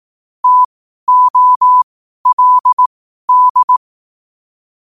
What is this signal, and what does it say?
Morse code, heard as a keyed tone, about 18 wpm, spaced as for 12 wpm, 1000 Hz. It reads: TOLD
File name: qotdfiles / 12wpm